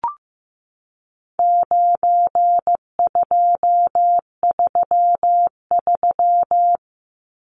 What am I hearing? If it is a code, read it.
Morse code, heard as a keyed tone, about 15 wpm, 700 Hz. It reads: 9233
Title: Numbers - 4-Digit 15wpm